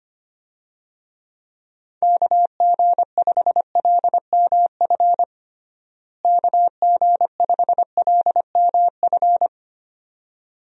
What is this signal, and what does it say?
Morse code, heard as a keyed tone, about 25 wpm, 700 Hz. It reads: KG5LMF KG5LMF
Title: USA Call Signs - 6 Characters - Repeated 2x - 25wpm